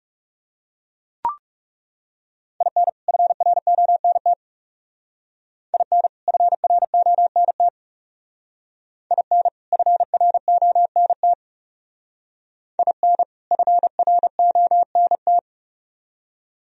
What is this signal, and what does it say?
Morse code, heard as a keyed tone, about 37 wpm, 700 Hz. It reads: IN FRONT IN FRONT IN FRONT IN FRONT
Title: Speed-Racing - Sets of 2 Words 30wpm